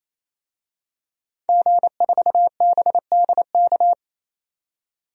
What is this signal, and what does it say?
Morse code, heard as a keyed tone, about 28 wpm, 700 Hz. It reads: G4BDK
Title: DXCC-100 Call Signs - 5 Characters - 28wpm